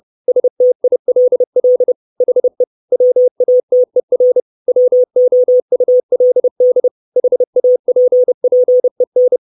Morse code, heard as a keyed tone, about 30 wpm, 500 Hz. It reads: STILL HE WATER WOULD HAPPEN
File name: 632 Words 30WPM